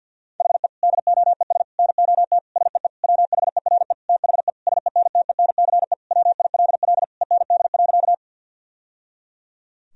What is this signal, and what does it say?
Morse code, heard as a keyed tone, about 50 wpm, 700 Hz. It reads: HE DOES NOT SEE WHERE THE SENTENCE WILL END.
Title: Sentences from Top 1000 Words - Review Part 1 - 50wpm